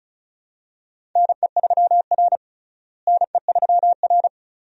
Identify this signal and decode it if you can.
Morse code, heard as a keyed tone, about 36 wpm, 700 Hz. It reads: NE3R NE3R
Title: USA Call Signs - 4 Characters - Repeated 2x - 35wpm